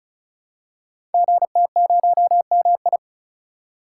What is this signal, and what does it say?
Morse code, heard as a keyed tone, about 34 wpm, 700 Hz. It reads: GT0MI
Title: Code Groups - 5 Letter:Numbers - 35wpm